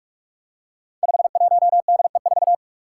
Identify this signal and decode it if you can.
Morse code, heard as a keyed tone, about 44 wpm, 700 Hz. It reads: 51DE4
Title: Five Challenging Letter-Number Combinations 45wpm